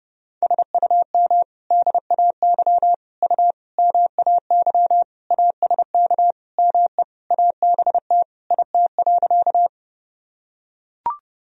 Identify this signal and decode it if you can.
Morse code, heard as a keyed tone, about 30 wpm, 700 Hz. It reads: SUM DAY U MAY ASK ME ABT IT.
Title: Sentences from Top 200 Words and Abbreviations 30wpm